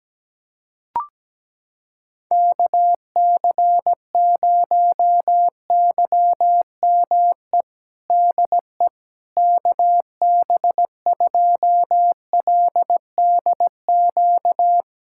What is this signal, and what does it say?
Morse code, heard as a keyed tone, about 17 wpm, 700 Hz. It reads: KC0YME DE KB2LDQ
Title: USA Callsign DE Callsign - 6 Characters - 17wpm